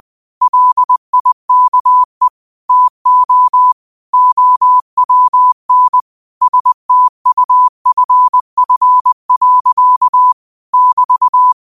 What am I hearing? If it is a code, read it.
Morse code, heard as a keyed tone, about 20 wpm, 1000 Hz. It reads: LIKE TO OWN STUFF. <BT>
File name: qotdfiles / 20wpm